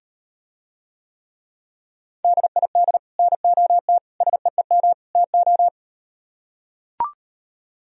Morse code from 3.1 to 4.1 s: NOT

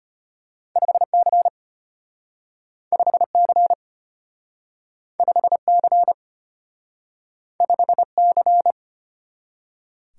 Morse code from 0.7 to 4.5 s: 5C 5C

